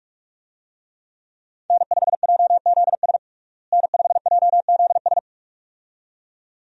Morse code, heard as a keyed tone, about 41 wpm, 700 Hz. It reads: N5JZS N5JZS
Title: USA Call Signs - 5 Characters - Repeated 2x - 45wpm